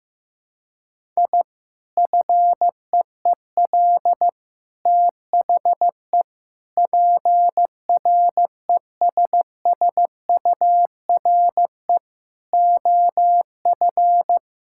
Morse code, heard as a keyed tone, about 15 wpm, 700 Hz. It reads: I FEEL THE PRESSURE OF
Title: Sentences from Top 1000 Words - Review Part 6 - 15wpm